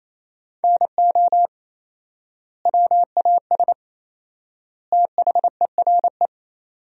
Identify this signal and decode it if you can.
Morse code, heard as a keyed tone, about 27 wpm, 700 Hz. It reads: NO WAS THERE
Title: 4x Word-Spacing - Sentences from Top 100 Words - 28wpm